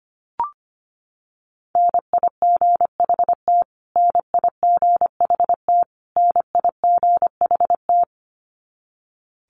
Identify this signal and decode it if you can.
Morse code, heard as a keyed tone, about 25 wpm, 700 Hz. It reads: NIGHT NIGHT NIGHT
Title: Top 500 Words Repeated 25wpm